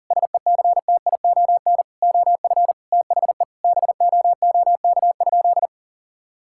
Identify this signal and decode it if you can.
Morse code, heard as a keyed tone, about 40 wpm, 700 Hz. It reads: SECTION OF THE BOOK?
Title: Sentences from Top 1000 Words - Review Part 5 - 40wpm